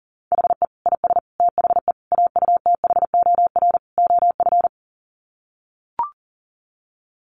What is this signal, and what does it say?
Morse code, heard as a keyed tone, about 40 wpm, 700 Hz. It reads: HE IS THE AUTHOR OF
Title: Sets of 5 Words Encore2 40wpm